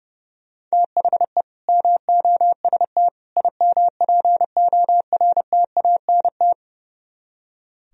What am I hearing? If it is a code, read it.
Morse code, heard as a keyed tone, about 35 wpm, 700 Hz. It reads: THE MOST IMPORTANT
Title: Speed-Racing - Sets of 3 Words 30wpm